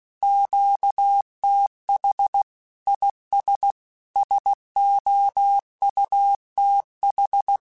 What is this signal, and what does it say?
Morse code, heard as a keyed tone, about 16 wpm, 800 Hz. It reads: QTH IS SOUTH
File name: QSOs / QSOs-mp3s / WA7QOX 2@15